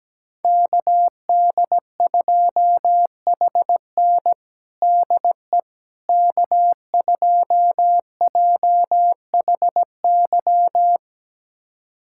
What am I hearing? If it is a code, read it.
Morse code, heard as a keyed tone, about 17 wpm, 700 Hz. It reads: KD2HN DE K2JHY
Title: USA Callsign DE Callsign - 5 Characters - 17wpm